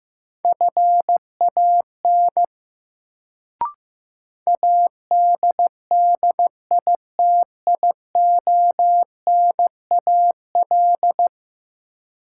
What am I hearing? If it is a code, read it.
Morse code, heard as a keyed tone, about 15 wpm, 700 Hz. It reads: FAN ADDITIONAL
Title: Sentences from Top 1000 Words - Review Part 4 - New Word Drill - 15wpm